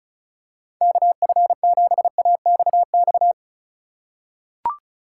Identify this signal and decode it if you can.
Morse code, heard as a keyed tone, about 34 wpm, 700 Hz. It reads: KF7AXX
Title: USA Call Signs - 6 Characters - 3x Character Spacing - 35wpm